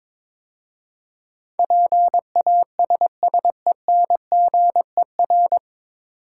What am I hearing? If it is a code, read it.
Morse code, heard as a keyed tone, about 22 wpm, 700 Hz. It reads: PASSENGER
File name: 9 Letter Words - First 3 Letters Given - 22wpm